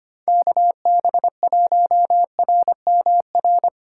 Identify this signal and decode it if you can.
Morse code, heard as a keyed tone, about 29 wpm, 700 Hz. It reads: KB1RMR
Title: Speed Racing - USA 2x3 Call Signs 25wpm